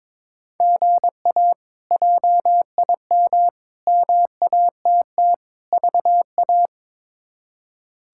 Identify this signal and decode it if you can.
Morse code, heard as a keyed tone, about 23 wpm, 700 Hz. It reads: GA JIM MATT VA
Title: K1USN SST QSO Practice - 22wpm